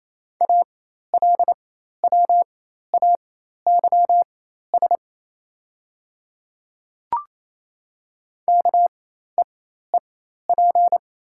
Morse code, heard as a keyed tone, about 28 wpm, spaced as for 13 wpm, 700 Hz. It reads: ALWAYS KEEP
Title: 4x Character Spacing - Top 300 Words - 28wpm